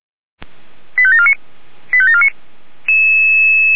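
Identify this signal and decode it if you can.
Five-tone selective call (ZVEI), heard as quick runs of five tones, tones of about 70 ms. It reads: 85749 85749
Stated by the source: ZVEI five-tone selective call received by SDR, from the Feuerwehr Orlamünde fire brigade alerting archive.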